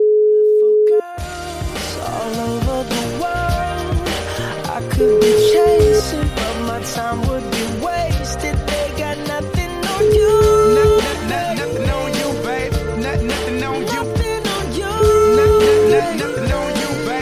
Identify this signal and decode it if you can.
Ringback tone (Europe, 425 Hz), with noise.